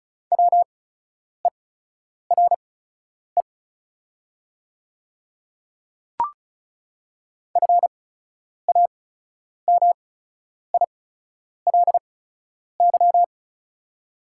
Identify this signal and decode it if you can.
Morse code, heard as a keyed tone, about 36 wpm, spaced as for 10 wpm, 690 Hz. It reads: WERE FAMILY
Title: 8x Character Spacing - Top 500 Words - 35wpm